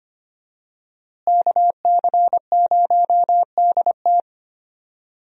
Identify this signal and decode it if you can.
Morse code, heard as a keyed tone, about 25 wpm, 700 Hz. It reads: KC0DT
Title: USA Call Signs - 5 Characters - Repeated 3x - 25wpm